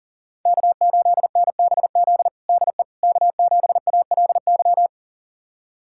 Morse code, heard as a keyed tone, about 37 wpm, 700 Hz. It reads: K8NBZ DE K7ALY
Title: USA Callsign DE Callsign - 5 Characters - 40wpm